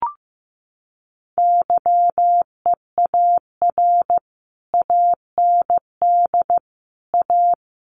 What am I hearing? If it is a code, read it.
Morse code, heard as a keyed tone, about 15 wpm, 700 Hz. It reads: YEAR AND A
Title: Sets of 5 Words Encore 15wpm